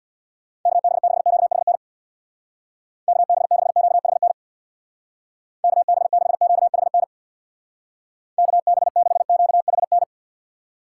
Morse code, heard as a keyed tone, about 63 wpm, 700 Hz. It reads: KB6QHN KB6QHN KB6QHN KB6QHN